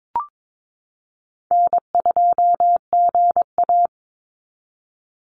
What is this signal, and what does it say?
Morse code, heard as a keyed tone, about 22 wpm, 700 Hz. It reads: N2GA
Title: Easy Call Signs Repeated 2x - 22wpm